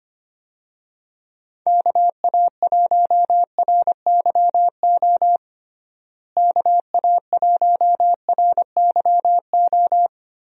Morse code, heard as a keyed tone, about 25 wpm, 700 Hz. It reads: KA1RYO KA1RYO
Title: USA Call Signs - 6 Characters - Repeated 3x - 25wpm